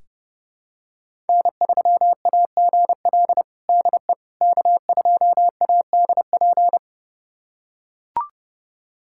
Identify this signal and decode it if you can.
Morse code, heard as a keyed tone, about 30 wpm, 700 Hz. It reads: N3AGL DE K2ADP
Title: USA Callsign DE Callsign - 5 Characters - 30wpm